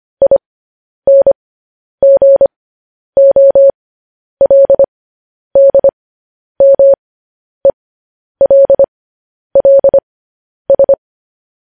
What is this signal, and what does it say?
Morse code, heard as a keyed tone, about 25 wpm, spaced as for 10 wpm, 550 Hz. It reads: INGOLDMELLS
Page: news headlines in morse code at 10 wpm